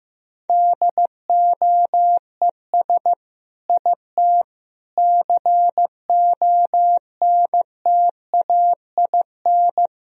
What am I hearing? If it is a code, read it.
Morse code, heard as a keyed tone, about 16 wpm, 700 Hz. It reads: DOES IT CONTAIN
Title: Sentences from Top 1000 Words - Review Part 2 - 15wpm